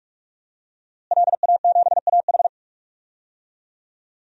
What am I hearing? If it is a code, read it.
Morse code, heard as a keyed tone, about 41 wpm, 700 Hz. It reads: LA7AH
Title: DXCC-100 Call Signs - 5 Characters - Repeated 2x - 45wpm